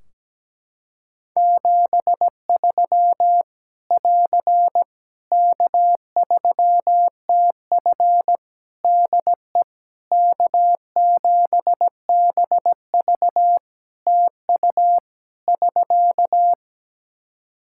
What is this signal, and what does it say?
Morse code, heard as a keyed tone, about 17 wpm, 700 Hz. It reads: 73 <AR> K3TF DE K7BV TU <SK>
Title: Practice QSO - Exchange 4 - Ending - 17wpm